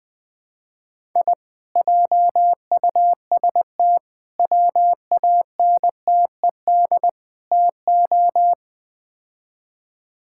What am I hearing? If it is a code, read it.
Morse code, heard as a keyed tone, about 21 wpm, 700 Hz. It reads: I JUST WANTED TO
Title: Build A Phrase - 4-Word Phrases - 20wpm